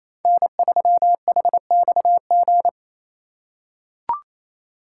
Morse code, heard as a keyed tone, about 28 wpm, 700 Hz. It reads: N3HXG
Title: USA Call Signs - 5 Characters - 2x Character Spacing - 28wpm